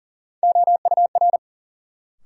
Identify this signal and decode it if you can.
Morse code, heard as a keyed tone, about 40 wpm, 700 Hz. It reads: OUR
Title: Sets of 2 Words - New Word Drill - 40wpm